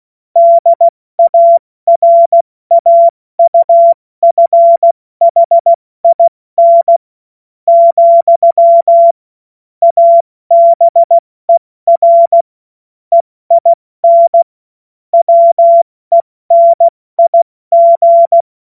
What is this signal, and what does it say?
Morse code, heard as a keyed tone, about 16 wpm, spaced as for 13 wpm, 680 Hz. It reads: DARAUFHIN, ABER EIN WENIG
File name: Morsekurs / Witze / Witz 14. Februar 2023 70 F80